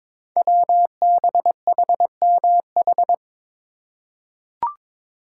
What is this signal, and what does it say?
Morse code, heard as a keyed tone, about 22 wpm, 700 Hz. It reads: WBHMH
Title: Code Groups - 5 Letter:Numbers - 22wpm